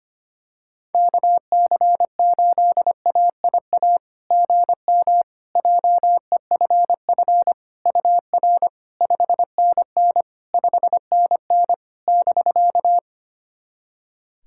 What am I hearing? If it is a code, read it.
Morse code, heard as a keyed tone, about 25 wpm, 700 Hz. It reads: KC8AIA GM JEFF UR 5NN 5NN <BK>
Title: SOTA QSO Practice 25wpm